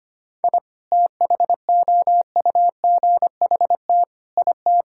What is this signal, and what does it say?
Morse code, heard as a keyed tone, about 25 wpm, 700 Hz. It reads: I THOUGHT IT
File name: Sets of 9 Words - 25wpm